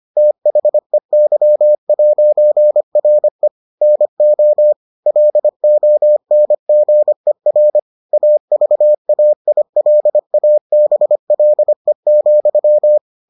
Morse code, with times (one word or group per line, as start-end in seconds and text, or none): 0.2-3.5 THEY'RE
3.8-4.7 NO
5.1-7.8 LONGER
8.1-13.0 AVAILABLE,